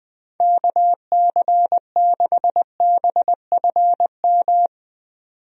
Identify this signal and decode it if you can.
Morse code, heard as a keyed tone, about 20 wpm, 700 Hz. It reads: KC6BFM
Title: USA Call Signs - 6 Characters - 20wpm